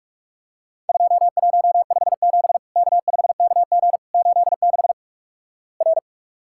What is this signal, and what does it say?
Morse code, heard as a keyed tone, about 46 wpm, 700 Hz. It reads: 2157 K5KG 86 R
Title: Worked All Europe DX Contest - QTC Exchange - 45wpm